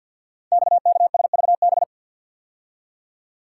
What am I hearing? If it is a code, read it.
Morse code, heard as a keyed tone, about 51 wpm, 690 Hz. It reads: XKSVB
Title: Code Groups - 5 Letters - 50wpm